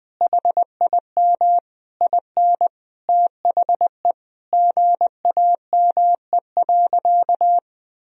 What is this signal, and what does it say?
Morse code, heard as a keyed tone, about 21 wpm, 700 Hz. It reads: HIM IN THE GAME.
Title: Sentences from Top 1000 Words - Review Part 3 - 20wpm